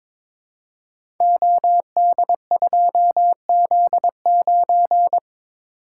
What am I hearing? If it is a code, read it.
Morse code, heard as a keyed tone, about 23 wpm, 700 Hz. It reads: OD2Z9
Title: Code Groups - 5 Letter:Numbers - 22wpm